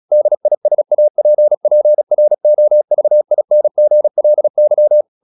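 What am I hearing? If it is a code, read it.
Morse code, heard as a keyed tone, about 36 wpm, 600 Hz. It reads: DISAPPROVINGLY
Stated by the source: Morse practice at 36 wpm, text of Ken Follett's novel Never, from CW med Gnister.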